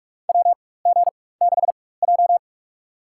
Morse code, heard as a keyed tone, about 44 wpm, spaced as for 21 wpm, 700 Hz. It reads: WG6J
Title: USA Call Signs - 4 Characters - 4x Character Spacing - 45wpm